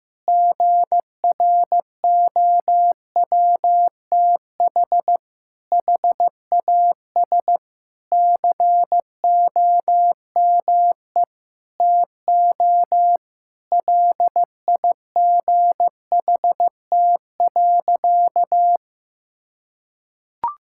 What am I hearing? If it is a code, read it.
Morse code, heard as a keyed tone, about 15 wpm, 700 Hz. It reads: GROWTH HAS COME TO LIGHT.
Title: Sentences from Top 1000 Words - Review Part 5 - 15wpm